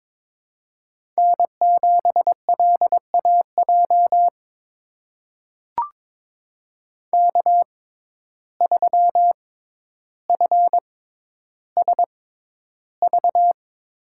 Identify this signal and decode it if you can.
Morse code, heard as a keyed tone, about 22 wpm, spaced as for 11 wpm, 700 Hz. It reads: N7LAJ K3FSV
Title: USA Call Signs - 5 Characters - 6x Character Spacing - 22wpm